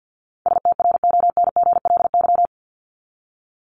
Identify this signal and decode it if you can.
Morse code, heard as a keyed tone, about 48 wpm, 700 Hz. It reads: STRONGLY
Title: LY Suffix - Top 100 Words with Base Words 50wpm